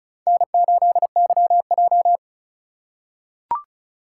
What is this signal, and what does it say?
Morse code, heard as a keyed tone, about 36 wpm, 700 Hz. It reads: N8YJ
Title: USA Call Signs - 4 Characters - 4x Character Spacing - 35wpm